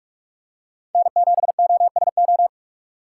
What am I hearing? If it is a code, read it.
Morse code, heard as a keyed tone, about 46 wpm, 700 Hz. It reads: N7OSO
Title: USA Call Signs - 5 Characters - 6x Character Spacing - 45wpm